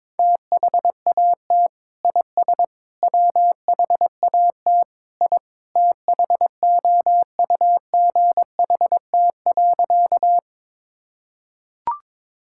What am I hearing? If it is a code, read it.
Morse code, heard as a keyed tone, about 22 wpm, 700 Hz. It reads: THAT IS WHAT I THOUGHT.
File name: practice / 22wpm / Sentences with 2-Word Phrases - 22wpm